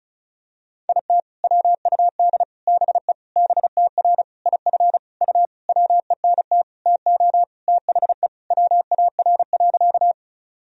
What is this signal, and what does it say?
Morse code, heard as a keyed tone, about 36 wpm, 700 Hz. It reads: IT WUD BE BTR IF U WENT TO THE WAR.
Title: Sentences from Top 400 Words and Abbreviations 35wpm